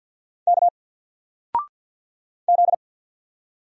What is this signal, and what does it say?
Morse code, heard as a keyed tone, about 48 wpm, 700 Hz. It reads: K Z
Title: Single Letter-Number 50wpm